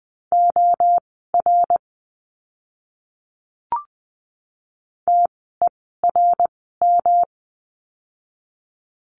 Morse code, heard as a keyed tone, about 20 wpm, spaced as for 14 wpm, 700 Hz. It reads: OR TERM
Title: 108 Lesson 18f - Letter m - Words 20wpm